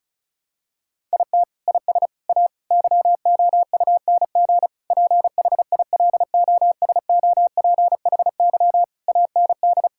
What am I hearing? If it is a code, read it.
Morse code, heard as a keyed tone, about 34 wpm, 700 Hz. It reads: IT IS A YOUNG PHILOSOPHY AND
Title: Sentences from Top 1000 Words - Review Part 6 - 35wpm